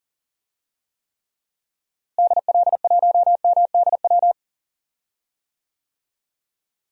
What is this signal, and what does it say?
Morse code, heard as a keyed tone, about 40 wpm, 700 Hz. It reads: DL1MDW
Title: DXCC-100 Call Signs - 6 Characters - Repeated 2x - 40wpm